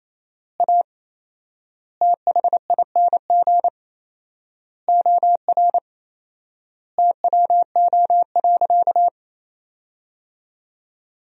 Morse code, heard as a keyed tone, about 28 wpm, 700 Hz. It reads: A THING OR TWO.
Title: 4x Word-Spacing - Sentences from Top 300 Words - 28wpm